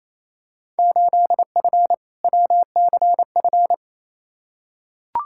Morse code, heard as a keyed tone, about 27 wpm, 700 Hz. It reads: 8F WCF
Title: Field Day Exchanges Encore - 28wpm